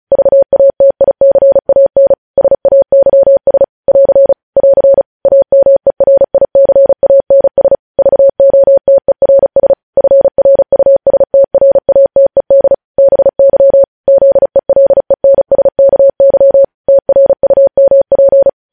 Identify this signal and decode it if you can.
Morse code, heard as a keyed tone, about 36 wpm, 550 Hz. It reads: VATICAN SAYS <AR> <AR> AMERICANS VOTERS FRUSTRATED BY ZELENSKY TRUMP
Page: News Headlines in Morse Code at 35 WPM